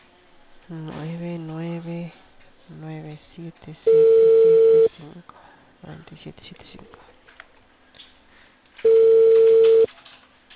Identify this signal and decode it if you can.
Ringback tone (China, 450 Hz), with speech.